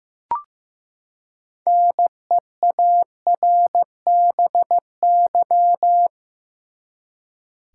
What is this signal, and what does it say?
Morse code, heard as a keyed tone, about 15 wpm, 700 Hz. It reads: NEARBY
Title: Six Letter Words - 15wpm